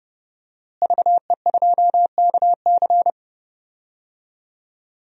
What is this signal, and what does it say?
Morse code, heard as a keyed tone, about 30 wpm, 700 Hz. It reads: VE2KC
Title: DXCC-100 Call Signs - 5 Characters - Repeated 3x - 30wpm